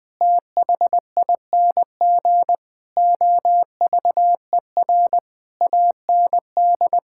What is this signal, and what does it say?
Morse code, heard as a keyed tone, about 20 wpm, 700 Hz. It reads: THING OVER AND
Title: Sentences with 7-Word Phrases - Spoken Beginning - 20wpm